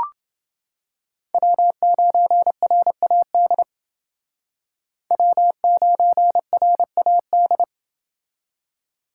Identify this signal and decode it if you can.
Morse code, heard as a keyed tone, about 28 wpm, 700 Hz. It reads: W9RAD W9RAD